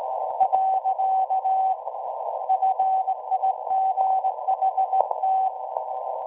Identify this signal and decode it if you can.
Morse code, heard as a keyed tone, about 18 wpm, 760 Hz. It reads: . FIGHT
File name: WebSDR records / SAQ at PI4THT part od 1005z 17-2kHz